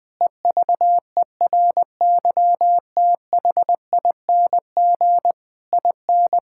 Morse code, heard as a keyed tone, about 20 wpm, 700 Hz. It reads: EVERYTHING IN
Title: Sets of 8 Words - 20wpm